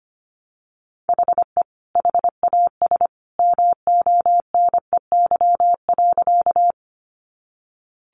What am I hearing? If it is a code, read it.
Morse code, heard as a keyed tone, about 25 wpm, 700 Hz. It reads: HE HAS MONEY.
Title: Sentences from Top 400 Words 25wpm